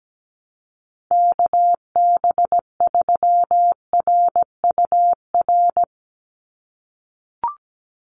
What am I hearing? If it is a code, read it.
Morse code, heard as a keyed tone, about 17 wpm, 700 Hz. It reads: KB3RUR
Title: USA Call Signs 17wpm